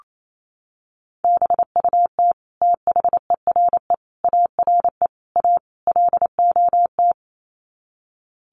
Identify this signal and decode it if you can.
Morse code, heard as a keyed tone, about 28 wpm, 700 Hz. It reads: BUT THERE ARE A LOT
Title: Sets of 5 Words Encore2 28wpm